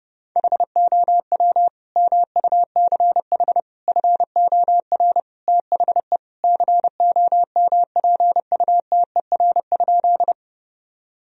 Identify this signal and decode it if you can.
Morse code, heard as a keyed tone, about 32 wpm, 700 Hz. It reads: HOW MUCH FOR THE COMPUTER?
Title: Sentences from Top 1000 Words - Review Part 3 - 30wpm